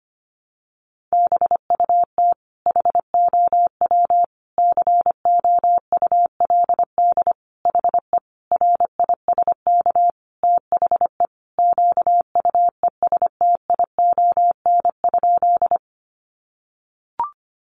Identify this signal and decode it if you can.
Morse code, heard as a keyed tone, about 25 wpm, 700 Hz. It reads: BUT HOW COULD HE RISK THE QUESTION?
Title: Sentences from Top 600 Words 25wpm